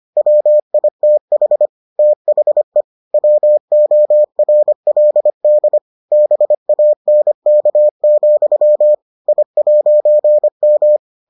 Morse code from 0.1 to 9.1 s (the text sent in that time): WITH THE WORLD BANK,